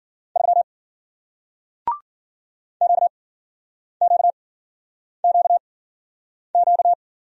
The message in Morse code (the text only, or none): 3 QQQQ